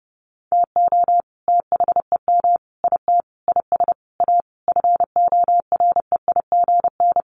Morse code, heard as a keyed tone, about 30 wpm, 700 Hz. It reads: TO THEM IT IS A FOREIGN
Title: Sentences from Top 1400 Words 30wpm